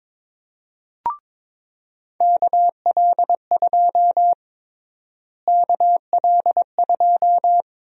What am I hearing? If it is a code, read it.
Morse code, heard as a keyed tone, about 23 wpm, 700 Hz. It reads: KL2 KL2
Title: USA Call Sign Prefixes - Repeated 2x - 22wpm